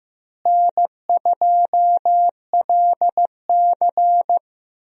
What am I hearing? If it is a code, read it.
Morse code, heard as a keyed tone, about 15 wpm, 700 Hz. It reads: N2LC
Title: USA Call Signs - 4 Characters - Repeated 3x - 15wpm